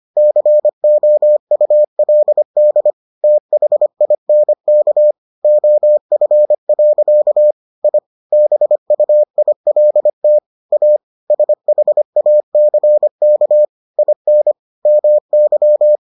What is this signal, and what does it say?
Morse code, heard as a keyed tone, about 25 wpm, 600 Hz. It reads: COULD THINK OF. I BUILT A SHACK IN MY